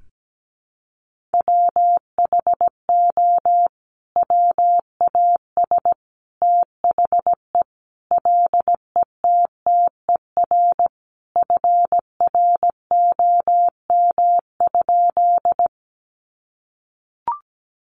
Morse code, heard as a keyed tone, about 17 wpm, 700 Hz. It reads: WHO WAS THE LETTER FROM?
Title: Sentences from Top 300 Words 17wpm